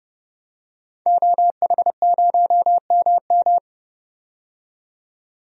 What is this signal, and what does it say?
Morse code, heard as a keyed tone, about 30 wpm, 700 Hz. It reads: OH0MM
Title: DXCC-100 Call Signs - 5 Characters - 30wpm